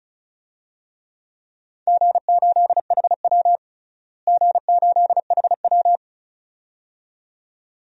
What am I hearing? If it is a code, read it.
Morse code, heard as a keyed tone, about 36 wpm, 700 Hz. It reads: G8HW G8HW
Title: DXCC-100 Call Signs - 4 Characters - Repeated 2x - 35wpm